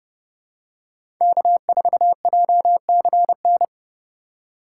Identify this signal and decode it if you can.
Morse code, heard as a keyed tone, about 31 wpm, 700 Hz. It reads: K4JCN
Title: USA Call Signs - 5 Characters - 3x Character Spacing - 30wpm